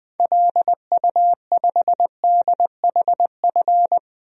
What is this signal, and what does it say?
Morse code, heard as a keyed tone, about 20 wpm, 700 Hz. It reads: LU5DHF
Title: DXCC-100 Call Signs - 6 Characters - Repeated 3x - 20wpm